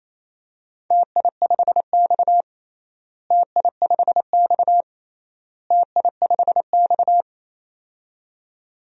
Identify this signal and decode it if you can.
Morse code, heard as a keyed tone, about 27 wpm, 700 Hz. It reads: TI5X TI5X TI5X
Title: DXCC-100 Call Signs - 4 Characters - Repeated 3x - 28wpm